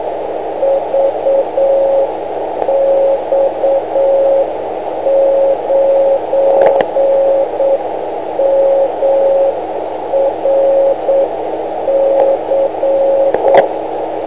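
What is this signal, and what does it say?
Morse code, heard as a keyed tone, about 8 wpm, 550 Hz, with radio noise. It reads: VX9MRC
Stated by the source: Morse from a 500 kHz transmitter.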